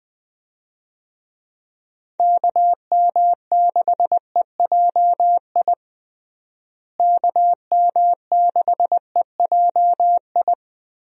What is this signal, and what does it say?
Morse code, heard as a keyed tone, about 21 wpm, 700 Hz. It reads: KM6EJI KM6EJI
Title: USA Call Signs - 6 Characters - Repeated 3x - 20wpm